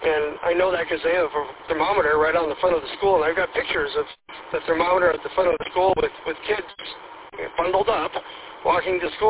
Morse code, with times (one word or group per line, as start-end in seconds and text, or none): none